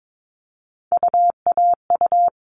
Morse code, heard as a keyed tone, about 22 wpm, 700 Hz. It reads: UAU
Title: All Cut Numbers - 3-Digit 22wpm